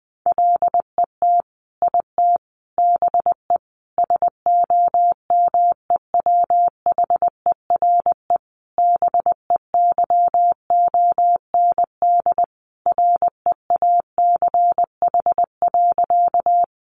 Morse code, heard as a keyed tone, about 20 wpm, 700 Hz. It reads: LET IT BE SOMEWHERE BEYOND REACH.
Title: Sentences from Top 1400 Words 20wpm